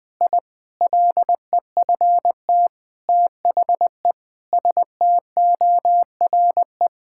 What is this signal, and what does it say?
Morse code, heard as a keyed tone, about 20 wpm, 700 Hz. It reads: I LEFT THE STORE
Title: Sentences from Top 1000 Words - Review Part 5 - 20wpm